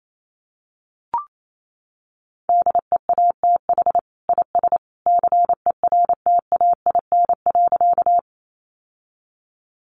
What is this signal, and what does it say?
Morse code, heard as a keyed tone, about 28 wpm, 700 Hz. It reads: DEATH IS CERTAIN.
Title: Sentences from Top 1100 Words 28wpm